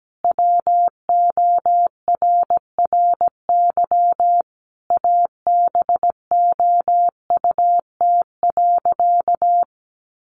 Morse code, heard as a keyed tone, about 17 wpm, 700 Hz. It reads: WORRY ABOUT.
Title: Sentences from Top 1100 Words 17wpm